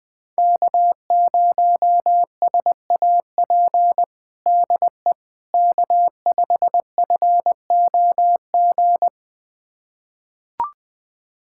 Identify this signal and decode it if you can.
Morse code, heard as a keyed tone, about 20 wpm, 700 Hz. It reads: K0SAP DE K5FOG
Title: USA Callsign DE Callsign - 5 Characters - 20wpm